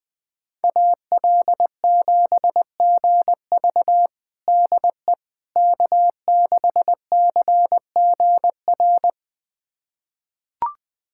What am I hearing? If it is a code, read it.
Morse code, heard as a keyed tone, about 20 wpm, 700 Hz. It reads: AL7GV DE K6CGR
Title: USA Callsign DE Callsign - 5 Characters - 20wpm